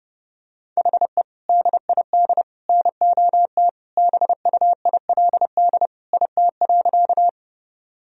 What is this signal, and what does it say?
Morse code, heard as a keyed tone, about 30 wpm, 700 Hz. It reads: HE DID NOT BUILD IT.
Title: Sentences from Top 1000 Words - Review Part 1 - 30wpm